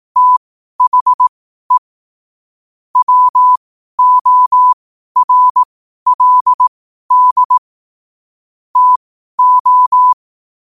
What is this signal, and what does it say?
Morse code, heard as a keyed tone, about 18 wpm, spaced as for 12 wpm, 1000 Hz. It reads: THE WORLD TO
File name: qotdfiles / 12wpm